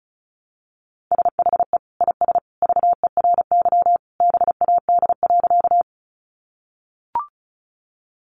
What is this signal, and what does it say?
Morse code, heard as a keyed tone, about 34 wpm, 700 Hz. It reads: SHE IS VERY BAD.
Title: Sentences from Top 700 Words 35wpm